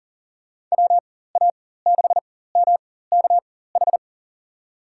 Morse code, heard as a keyed tone, about 40 wpm, spaced as for 19 wpm, 700 Hz. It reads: WA6MKH
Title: USA Call Signs - 4x Character Spacing - 40wpm